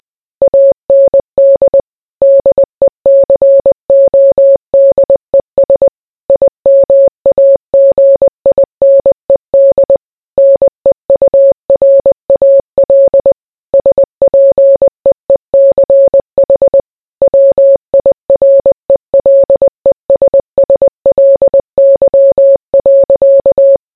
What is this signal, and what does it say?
Morse code, heard as a keyed tone, about 20 wpm, 550 Hz. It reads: AND DECODES IMAGINED NEURAL SPEECH WIRELESSLY.